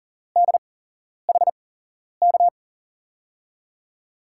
Morse code, heard as a keyed tone, about 39 wpm, spaced as for 11 wpm, 700 Hz. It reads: DHK
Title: USA Call Signs - 5 Characters - 8x Character Spacing - 40wpm